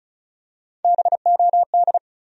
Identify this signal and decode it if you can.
Morse code, heard as a keyed tone, about 34 wpm, 700 Hz. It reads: BOD
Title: USA Call Sign Postfixes - 3 Letters 35wpm